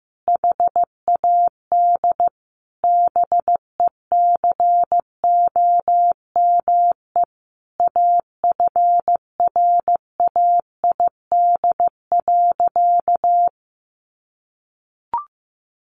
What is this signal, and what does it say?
Morse code, heard as a keyed tone, about 15 wpm, 700 Hz. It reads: HAD BECOME AFRAID.
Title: Sentences from Top 1500 Words 15wpm